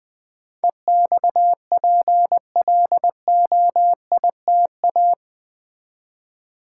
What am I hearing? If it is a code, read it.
Morse code, heard as a keyed tone, about 20 wpm, 700 Hz. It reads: EXPLOITA
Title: Build A Word - 4 Letters At A Time - 12-Letter Words - 20wpm